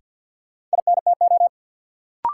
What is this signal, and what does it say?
Morse code, heard as a keyed tone, about 50 wpm, 700 Hz. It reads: INTO
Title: Sentences from Top 600 Words - New Word Drill - 50wpm